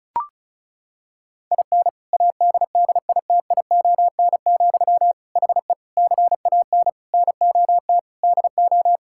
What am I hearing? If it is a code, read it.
Morse code, heard as a keyed tone, about 36 wpm, 690 Hz. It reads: IN ADDITION, HE CAN NOT DO
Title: Sentences from Top 1000 Words - Review Part 6 - 35wpm